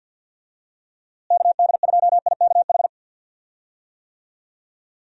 Morse code, heard as a keyed tone, about 51 wpm, 700 Hz. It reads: KD2IKH
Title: USA Call Signs - 6 Characters - Repeated 2x - 50wpm